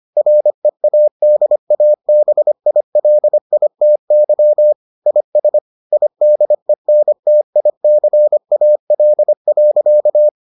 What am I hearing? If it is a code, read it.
Morse code, heard as a keyed tone, about 25 wpm, 600 Hz. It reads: READABILITY IS IDENTICAL.